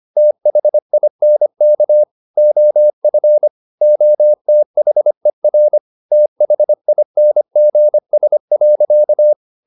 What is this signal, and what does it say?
Morse code, heard as a keyed tone, about 25 wpm, 600 Hz. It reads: THINK OF OTHER THINGS.
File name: the old man and the sea part 1 25wpm